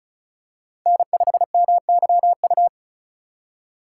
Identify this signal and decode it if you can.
Morse code, heard as a keyed tone, about 36 wpm, 700 Hz. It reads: N5MYU
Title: USA Call Signs - 5 Characters - 4x Character Spacing - 35wpm